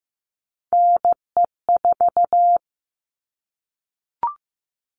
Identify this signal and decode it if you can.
Morse code, heard as a keyed tone, about 15 wpm, 700 Hz. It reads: NE4
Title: All Cut Numbers - 3-Digit 15wpm